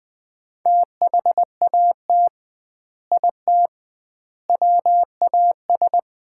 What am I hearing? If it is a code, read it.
Morse code, heard as a keyed tone, about 21 wpm, 700 Hz. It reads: THAT IT WAS
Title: 2x Word-Spacing - Sets of 3 Words - 20wpm